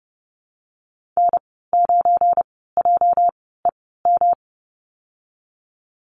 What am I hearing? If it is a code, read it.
Morse code, heard as a keyed tone, about 30 wpm, spaced as for 17 wpm, 700 Hz. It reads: N9JEM
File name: Lesson 37g - Letter x - Call Signs 30wpm